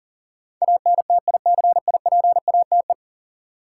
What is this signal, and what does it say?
Morse code, heard as a keyed tone, about 40 wpm, 700 Hz. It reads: ANTICIPATE
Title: Sentences from Top 1500 Words - New Word Drill Alt - 40wpm